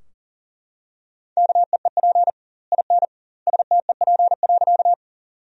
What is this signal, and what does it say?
Morse code, heard as a keyed tone, about 40 wpm, 700 Hz. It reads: KEEP IN STEP.
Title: 2x Word-Spacing - Sentences from Top 400 Words - 40wpm